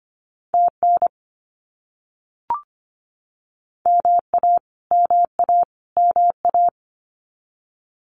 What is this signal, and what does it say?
Morse code, heard as a keyed tone, about 25 wpm, 700 Hz. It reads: TN MA MA MA
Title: US States Repeated 3x 25wpm